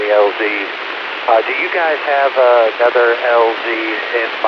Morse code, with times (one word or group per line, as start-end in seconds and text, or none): none